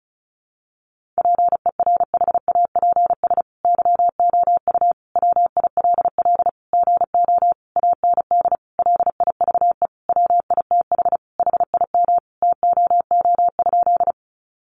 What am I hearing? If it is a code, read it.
Morse code, heard as a keyed tone, about 34 wpm, 700 Hz. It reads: PERHAPS YOU WILL GO AND LIVE WITH HIM TOO?
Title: Sentences from Top 500 Words 35wpm